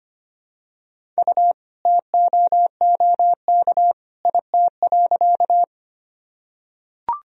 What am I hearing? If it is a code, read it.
Morse code, heard as a keyed tone, about 25 wpm, 700 Hz. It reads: U TOOK IT.
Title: Sentences from Top 300 Words and Abbreviations 25wpm